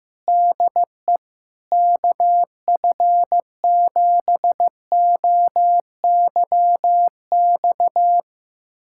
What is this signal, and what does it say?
Morse code, heard as a keyed tone, about 15 wpm, 700 Hz. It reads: DE KF7OYX
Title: USA Callsign DE Callsign - 6 Characters - 15wpm